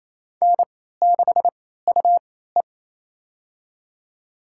Morse code, heard as a keyed tone, about 28 wpm, spaced as for 16 wpm, 700 Hz. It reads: N6UE
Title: USA Call Signs - 4 Characters - 3x Character Spacing - 28wpm